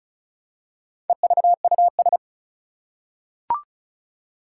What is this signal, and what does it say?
Morse code, heard as a keyed tone, about 36 wpm, 700 Hz. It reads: EVUS
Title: Four Challenging Letter-Number Combinations 35wpm